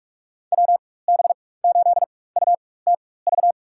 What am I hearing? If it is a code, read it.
Morse code, heard as a keyed tone, about 44 wpm, spaced as for 21 wpm, 700 Hz. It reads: WB8UTV
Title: USA Call Signs - 6 Characters - 4x Character Spacing - 45wpm